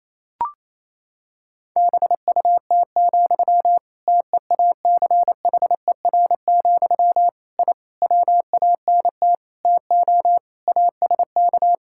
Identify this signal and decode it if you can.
Morse code, heard as a keyed tone, about 28 wpm, 700 Hz. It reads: BUT, TEACHER, I WANT TO ASK
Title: Sentences from Top 1000 Words - Review Part 6 - 28wpm